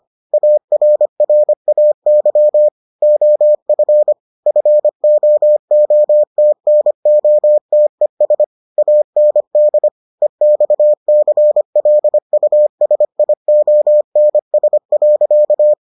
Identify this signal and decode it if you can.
Morse code, heard as a keyed tone, about 25 wpm, 600 Hz. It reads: ARRAY OF FOOTNOTES AND EXCLUSIONS.